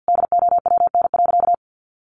Morse code, heard as a keyed tone, about 50 wpm, 700 Hz. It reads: DOWN.
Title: Sentences from Top 1100 Words 50wpm